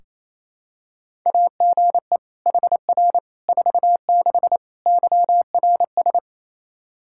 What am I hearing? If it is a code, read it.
Morse code, heard as a keyed tone, about 27 wpm, 700 Hz. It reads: AGE HR 46 YRS
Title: Practice QSO - Exchange 3 - More Info - 28wpm